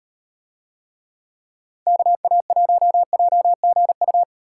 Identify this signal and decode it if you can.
Morse code, heard as a keyed tone, about 36 wpm, 700 Hz. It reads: KA1JGU